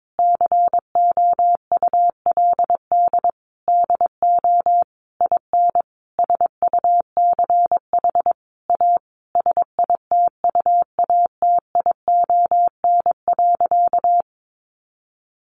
Morse code, heard as a keyed tone, about 22 wpm, 700 Hz. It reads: COULD DO IN SUCH A SITUATION.
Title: Sentences from Top 1200 Words 22wpm